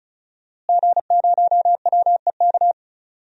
Morse code, heard as a keyed tone, about 34 wpm, 700 Hz. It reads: G0WEK
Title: DXCC-100 Call Signs - 5 Characters - Repeated 2x - 35wpm